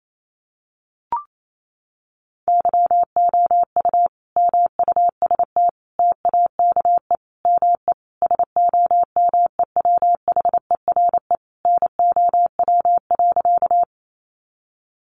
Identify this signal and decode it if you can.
Morse code, heard as a keyed tone, about 28 wpm, 700 Hz. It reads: YOU MUST TAKE ME SOMEWHERE NOW.
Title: Sentences from Top 1400 Words 28wpm